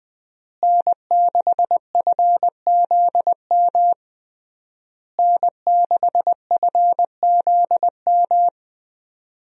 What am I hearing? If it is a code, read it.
Morse code, heard as a keyed tone, about 21 wpm, 700 Hz. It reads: N6FZM N6FZM
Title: USA Call Signs - 5 Characters - Repeated 3x - 20wpm